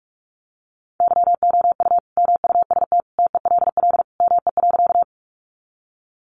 Morse code, heard as a keyed tone, about 44 wpm, 700 Hz. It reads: YOU MUST TELL ME.
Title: Sentences from Top 200 Words 45wpm